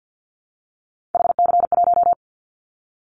Morse code, heard as a keyed tone, about 50 wpm, 700 Hz. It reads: HC1
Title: Callsign Prefixes for Top 100 DXCC - 50wpm